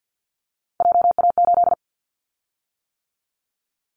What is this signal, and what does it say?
Morse code, heard as a keyed tone, about 50 wpm, 700 Hz. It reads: JA8
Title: Callsign Prefixes for Top 50 DXCC - 50wpm